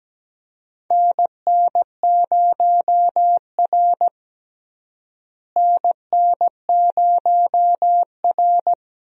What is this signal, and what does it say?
Morse code, heard as a keyed tone, about 17 wpm, 700 Hz. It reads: NN0R NN0R
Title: USA Call Signs - 4 Characters - Repeated 2x - 17wpm